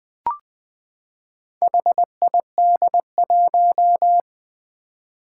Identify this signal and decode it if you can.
Morse code, heard as a keyed tone, about 20 wpm, 700 Hz. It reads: HID1
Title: Four Challenging Letter-Number Combinations 20wpm